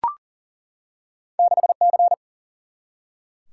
Morse code, heard as a keyed tone, about 38 wpm, 700 Hz. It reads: 6C
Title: Single-Letter Single-Number 40wpm